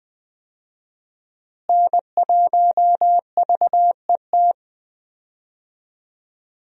Morse code, heard as a keyed tone, about 21 wpm, 700 Hz. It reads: N1VET
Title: USA Call Signs - 5 Characters - Repeated 3x - 20wpm